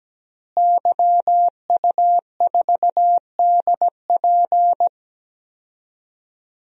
Morse code, heard as a keyed tone, about 17 wpm, 700 Hz. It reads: YU4DP
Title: DXCC-100 Call Signs - 5 Characters - Repeated 2x - 17wpm